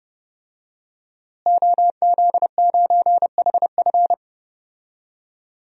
Morse code, heard as a keyed tone, about 32 wpm, 700 Hz. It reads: OZ9HF
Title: DXCC-100 Call Signs - 5 Characters - Repeated 3x - 30wpm